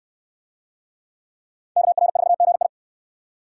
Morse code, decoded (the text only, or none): KR4GI